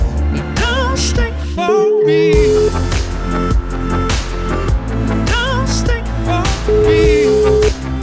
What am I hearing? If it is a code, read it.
Ringback tone (Europe, 425 Hz), with noise.